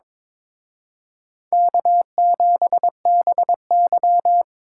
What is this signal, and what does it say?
Morse code, heard as a keyed tone, about 26 wpm, 700 Hz. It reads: K7BY
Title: Speed Racing - USA 1x2 Call Signs 22wpm